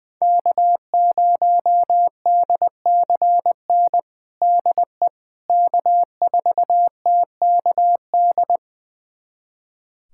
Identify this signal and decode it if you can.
Morse code, heard as a keyed tone, about 20 wpm, 700 Hz. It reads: K0DCN DE K4TKD
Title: USA Callsign DE Callsign - 5 Characters - 20wpm